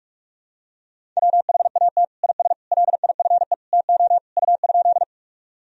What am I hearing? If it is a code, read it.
Morse code, heard as a keyed tone, about 46 wpm, 700 Hz. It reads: WHAT IS LIFE TO U?
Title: Sentences from Top 300 Words and Abbreviations 45wpm